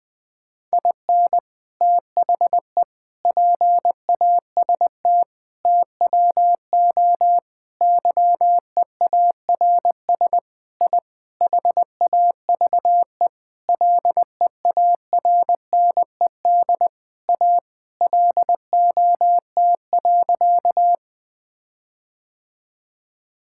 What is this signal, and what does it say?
Morse code, heard as a keyed tone, about 20 wpm, 700 Hz. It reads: IN THE PAST TWO YEARS I HAVE LEARNED A LOT.
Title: Sentences with 5-Word Phrases - Spoken Beginning - 20wpm